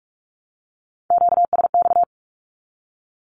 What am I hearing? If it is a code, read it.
Morse code, heard as a keyed tone, about 46 wpm, 700 Hz. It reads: QSX
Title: Q-Codes 45wpm